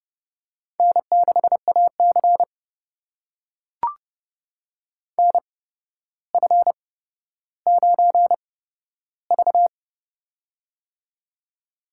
Morse code, heard as a keyed tone, about 30 wpm, spaced as for 13 wpm, 700 Hz. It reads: N6AC NF9V
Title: USA Call Signs - 4 Characters - 8x Character Spacing - 30wpm